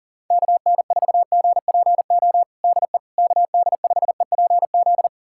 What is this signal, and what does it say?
Morse code, heard as a keyed tone, about 40 wpm, 700 Hz. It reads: KN4GPO DE KD5EPZ
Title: USA Callsign DE Callsign - 6 Characters - 40wpm